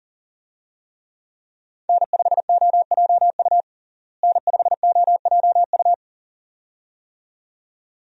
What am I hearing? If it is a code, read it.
Morse code, heard as a keyed tone, about 40 wpm, 700 Hz. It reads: N5OJU N5OJU
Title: USA Call Signs - 5 Characters - Repeated 2x - 40wpm